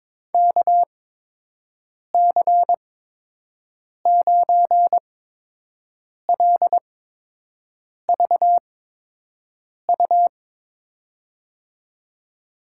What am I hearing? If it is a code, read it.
Morse code, heard as a keyed tone, about 22 wpm, spaced as for 6 wpm, 700 Hz. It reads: KC9LVU
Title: USA Call Signs - 6 Characters - 8x Character Spacing - 22wpm